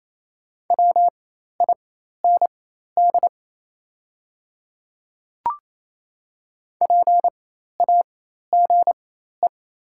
Morse code, heard as a keyed tone, about 28 wpm, spaced as for 13 wpm, 690 Hz. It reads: WIND PAGE
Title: 4x Character Spacing - Top 500 Words - 28wpm